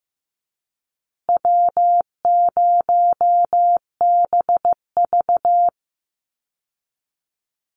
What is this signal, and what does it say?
Morse code, heard as a keyed tone, about 14 wpm, 700 Hz. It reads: W0BV
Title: Easy Call Signs Repeated 3x - 15wpm